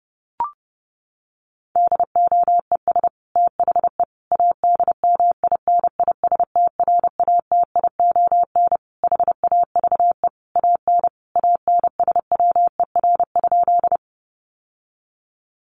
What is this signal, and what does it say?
Morse code, heard as a keyed tone, about 28 wpm, 700 Hz. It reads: DOES THE ADMINISTRATION HAVE AN ANSWER?
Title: Sentences from Top 1300 Words 30wpm